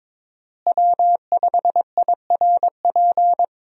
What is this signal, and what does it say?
Morse code, heard as a keyed tone, about 22 wpm, 700 Hz. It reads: W5IRP
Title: USA Call Signs - 5 Characters - Repeated 3x - 22wpm